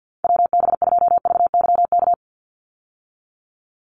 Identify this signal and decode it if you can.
Morse code, heard as a keyed tone, about 50 wpm, 700 Hz. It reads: WB2VYK
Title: Call Signs Repeated 2x - 50wpm